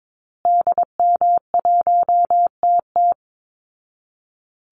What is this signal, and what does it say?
Morse code, heard as a keyed tone, about 22 wpm, 700 Hz. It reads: DM1TT
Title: Call Signs Repeated 3x - 22wpm